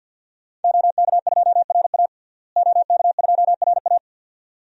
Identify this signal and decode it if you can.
Morse code, heard as a keyed tone, about 51 wpm, 700 Hz. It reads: OK2RA OK2RA
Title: DXCC-100 Call Signs - 5 Characters - Repeated 2x - 50wpm